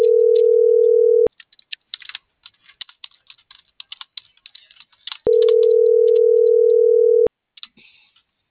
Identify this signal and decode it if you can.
Ringback tone (North America), 440 + 480 Hz.